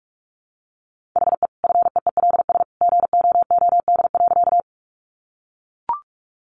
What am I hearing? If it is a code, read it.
Morse code, heard as a keyed tone, about 44 wpm, 700 Hz. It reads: HE FEELS GOOD.
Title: Additional Easy Phrases 45wpm